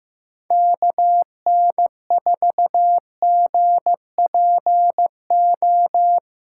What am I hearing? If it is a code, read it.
Morse code, heard as a keyed tone, about 16 wpm, 700 Hz. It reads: KN4GPO
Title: USA Callsign DE Callsign - 6 Characters - 15wpm